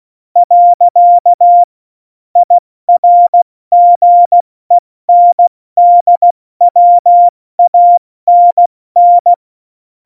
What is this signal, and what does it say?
Morse code, heard as a keyed tone, about 16 wpm, 700 Hz. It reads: . IRGENDWANN
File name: Morsekurs / Witze / Witz 29. Mai 2018, 70 F80